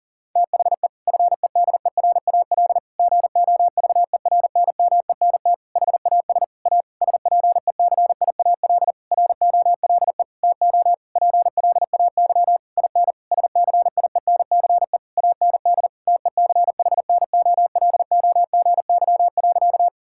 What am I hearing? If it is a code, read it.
Morse code, heard as a keyed tone, about 37 wpm, 700 Hz. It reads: THE FEDERAL GOVERNMENT HAS A SPECIAL ROLE TO PLAY IN SCIENCE AND TECHNOLOGY.
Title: Sentences from Top 1000 Words - Review Part 6 - 40wpm